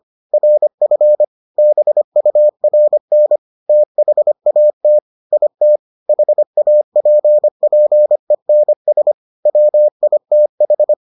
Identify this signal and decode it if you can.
Morse code, heard as a keyed tone, about 25 wpm, 600 Hz. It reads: RF BURN THAT IT HAPPENS WITH